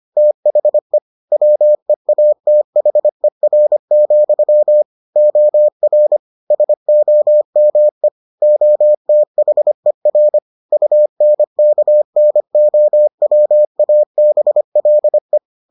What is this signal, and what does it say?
Morse code, heard as a keyed tone, about 25 wpm, 600 Hz. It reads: THE WEATHER, OR SOME OTHER UNKNOWABLE